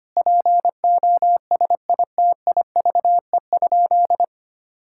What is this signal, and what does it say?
Morse code, heard as a keyed tone, about 25 wpm, 700 Hz. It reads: POSITIVE?
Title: Sentences from Top 1000 Words - Review Part 5 - 25wpm